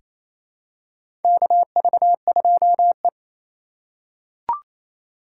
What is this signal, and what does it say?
Morse code, heard as a keyed tone, about 28 wpm, 700 Hz. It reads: KV2E
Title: USA Call Signs - 4 Characters - 2x Character Spacing - 28wpm